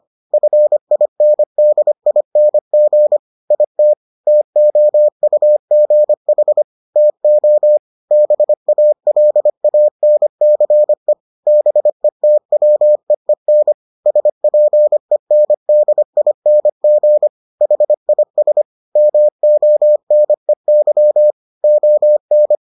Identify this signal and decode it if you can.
Morse code, heard as a keyed tone, about 25 wpm, 600 Hz. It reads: FINDING IT TOUGH TO BALANCE BETWEEN SPENDING HIS MONEY ON